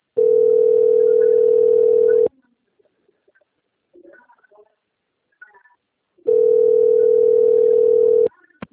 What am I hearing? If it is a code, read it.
Ringback tone (North America), 440 + 480 Hz.